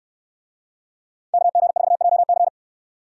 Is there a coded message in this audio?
Yes: Morse code from 1.3 to 2.5 s, heard as a keyed tone, about 64 wpm, 700 Hz. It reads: KG4JC